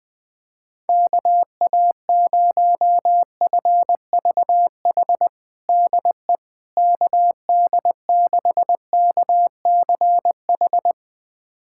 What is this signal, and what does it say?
Morse code, heard as a keyed tone, about 21 wpm, 700 Hz. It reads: KA0FVH DE KD6KCH
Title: USA Callsign DE Callsign - 6 Characters - 20wpm